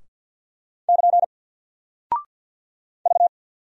Morse code, heard as a keyed tone, about 44 wpm, 700 Hz. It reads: <KN> V
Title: Single Character (Letter:Number:Punctuation:Prosign) 50wpm